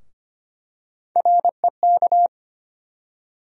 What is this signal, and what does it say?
Morse code, heard as a keyed tone, about 25 wpm, 700 Hz. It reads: REK
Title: USA Call Sign Postfixes - 3 Letters 25wpm